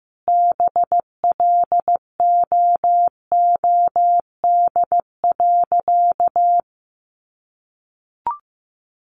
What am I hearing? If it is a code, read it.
Morse code, heard as a keyed tone, about 15 wpm, 700 Hz. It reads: BLOOD.
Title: Sentences from Top 1200 Words 15wpm